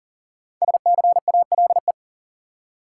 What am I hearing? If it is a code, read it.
Morse code, heard as a keyed tone, about 40 wpm, 700 Hz. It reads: SCALE
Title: Five Letter Words - 40wpm